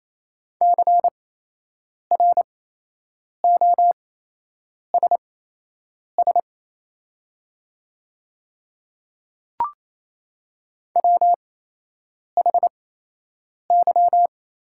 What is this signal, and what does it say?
Morse code, heard as a keyed tone, about 27 wpm, spaced as for 8 wpm, 690 Hz. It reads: CROSS WHY
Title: 8x Character Spacing - Top 500 Words - 28wpm